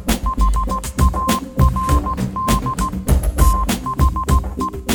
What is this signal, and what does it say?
Morse code, heard as a keyed tone, about 16 wpm, 1050 Hz. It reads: HARD THE
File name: Kirkoid / Kirkoid - Small Stuff (Morse Mix)